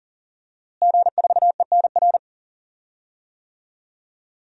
Morse code, heard as a keyed tone, about 40 wpm, 700 Hz. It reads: G4ENR
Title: DXCC-100 Call Signs - 5 Characters - Repeated 3x - 40wpm